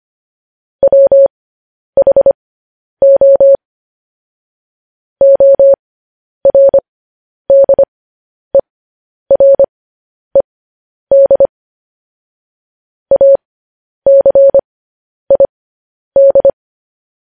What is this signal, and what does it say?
Morse code, heard as a keyed tone, about 25 wpm, spaced as for 10 wpm, 550 Hz. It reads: WHO ORDERED ACID
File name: podcast / News headlines-1x-25x10